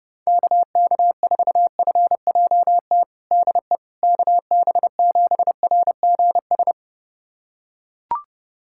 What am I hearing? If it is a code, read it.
Morse code, heard as a keyed tone, about 30 wpm, 700 Hz. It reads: KK4FJT DE KB7RGS
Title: USA Callsign DE Callsign - 6 Characters - 30wpm